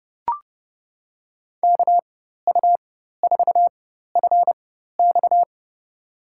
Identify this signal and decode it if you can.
Morse code, heard as a keyed tone, about 29 wpm, spaced as for 14 wpm, 700 Hz. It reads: KU4FX
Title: USA Call Signs - 5 Characters - 4x Character Spacing - 30wpm